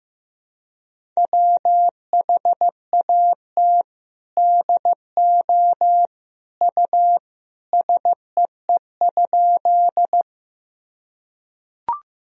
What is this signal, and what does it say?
Morse code, heard as a keyed tone, about 15 wpm, 700 Hz. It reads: WHAT DO U SEE?
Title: Sentences from Top 100 Words and Abbreviations 15wpm